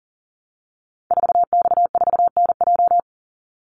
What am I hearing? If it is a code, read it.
Morse code, heard as a keyed tone, about 40 wpm, 700 Hz. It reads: 4X4NJ
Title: Call Signs - 40wpm